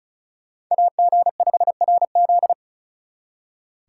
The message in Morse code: AG5RZ

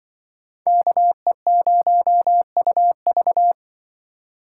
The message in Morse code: KE0UV

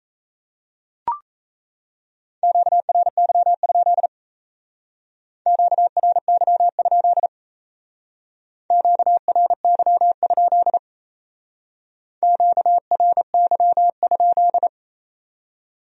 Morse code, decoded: QRY? QRY? QRY? QRY?

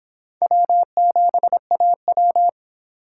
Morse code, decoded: W7AW

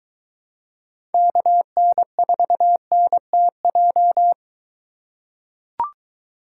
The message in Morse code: KN4NTJ

